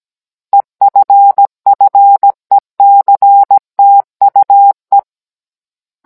Morse code, heard as a keyed tone, about 17 wpm, 800 Hz. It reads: EFFECTUE
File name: Cours-NR19-16mm